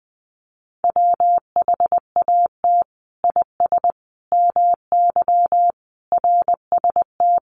Morse code, heard as a keyed tone, about 20 wpm, 700 Hz. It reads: WHAT IS MY RST
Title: Common QSO Phrases 20wpm